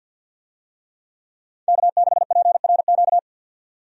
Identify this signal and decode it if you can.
Morse code, heard as a keyed tone, about 52 wpm, 700 Hz. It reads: K6PRQ
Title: USA Call Signs - 5 Characters - 50wpm